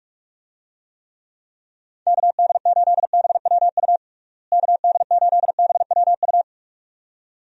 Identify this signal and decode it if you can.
Morse code, heard as a keyed tone, about 46 wpm, 700 Hz. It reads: KD8BWU KD8BWU
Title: USA Call Signs - 6 Characters - Repeated 2x - 45wpm